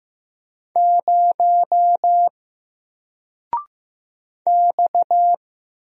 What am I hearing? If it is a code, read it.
Morse code, heard as a keyed tone, about 15 wpm, 700 Hz. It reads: 0 X
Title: Single Letter-Number Alt 15wpm